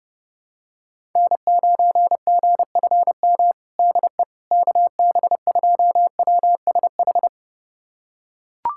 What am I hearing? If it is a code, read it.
Morse code, heard as a keyed tone, about 32 wpm, 700 Hz. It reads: N9GFM DE KB2WSH
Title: USA Callsign DE Callsign 30wpm